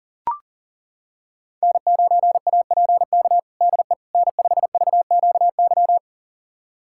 Morse code, heard as a keyed tone, about 37 wpm, 700 Hz. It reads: N9APK DE N5VQY
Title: USA Callsign DE Callsign - 5 Characters - 40wpm